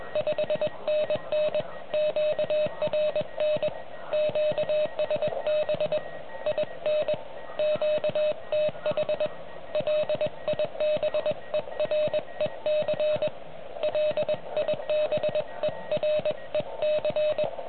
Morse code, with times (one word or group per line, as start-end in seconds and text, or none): 0.2-1.6 5NN
1.9-3.7 QRN
4.1-6.0 QSB
6.5-7.1 IN
7.6-9.3 QTH
9.8-13.3 LIBEREC
13.8-17.4 LIBEREC